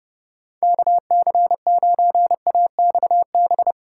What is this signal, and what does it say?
Morse code, heard as a keyed tone, about 40 wpm, spaced as for 35 wpm, 700 Hz. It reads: KC9AXB
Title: Speed-Racing Common QSO Phrases Encore - 30wpm